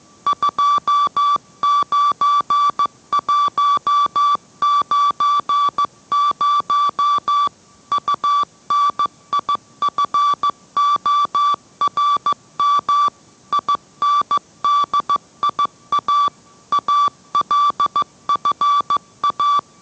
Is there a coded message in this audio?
Yes: Morse code from 0.3 to 19.6 s, heard as a keyed tone, about 15 wpm, 1160 Hz. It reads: 29190 UNIFORM INDIA ALFA